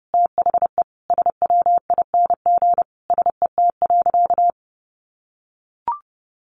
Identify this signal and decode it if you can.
Morse code, heard as a keyed tone, about 30 wpm, 700 Hz. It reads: THE SWING SET.
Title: Additional Easy Phrases 30wpm